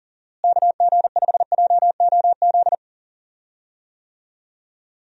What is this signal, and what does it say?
Morse code, heard as a keyed tone, about 40 wpm, 700 Hz. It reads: KG5JOZ
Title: USA Call Signs - 6 Characters - Repeated 2x - 40wpm